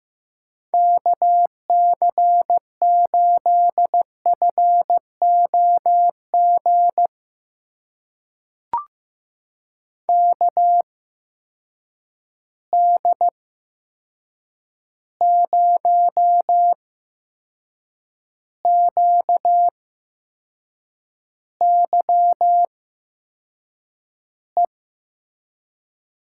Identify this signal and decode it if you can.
Morse code, heard as a keyed tone, about 15 wpm, spaced as for 6 wpm, 700 Hz. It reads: KC8FOG KD0QYE
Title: USA Call Signs - 6 Characters - 8x Character Spacing - 15wpm